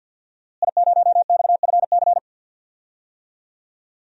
Morse code, heard as a keyed tone, about 50 wpm, 700 Hz. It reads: I0XFC
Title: DXCC-100 Call Signs - 5 Characters - Repeated 3x - 50wpm